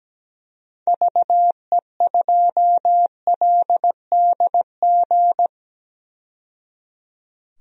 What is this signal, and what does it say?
Morse code, heard as a keyed tone, about 17 wpm, 700 Hz. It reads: VE2LDG
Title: DXCC-100 Call Signs - 6 Characters - 17wpm